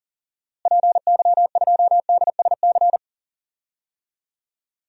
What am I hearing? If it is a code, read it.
Morse code, heard as a keyed tone, about 40 wpm, 700 Hz. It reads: PY2DSC
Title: DXCC-100 Call Signs - 6 Characters - Repeated 2x - 40wpm